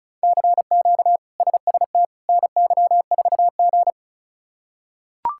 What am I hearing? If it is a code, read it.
Morse code, heard as a keyed tone, about 36 wpm, 700 Hz. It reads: CQ SST NY4G
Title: K1USN SST QSO Practice - 35wpm